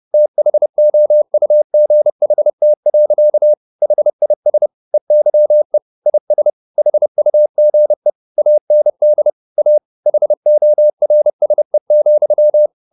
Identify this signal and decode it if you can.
Morse code, heard as a keyed tone, about 30 wpm, 600 Hz. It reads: THOUGHT. HIS EYE IS HUGE AND A HORSE,
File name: the old man and the sea / the old man and the sea 1 30wpm